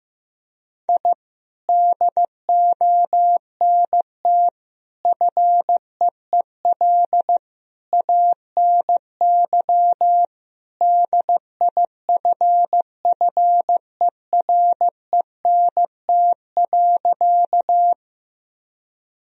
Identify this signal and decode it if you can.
Morse code, heard as a keyed tone, about 15 wpm, 700 Hz. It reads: I DONT FEEL ANY DIFFERENT.
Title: Sentences from Top 1000 Words - Review Part 3 - 15wpm